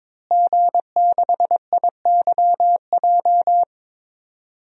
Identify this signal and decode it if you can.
Morse code, heard as a keyed tone, about 22 wpm, 700 Hz. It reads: G6IYJ
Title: DXCC-100 Call Signs - 5 Characters - Repeated 3x - 22wpm